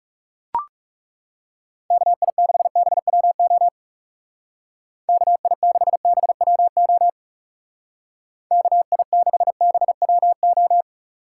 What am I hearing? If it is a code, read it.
Morse code, heard as a keyed tone, about 38 wpm, 700 Hz. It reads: KI6BWO KI6BWO KI6BWO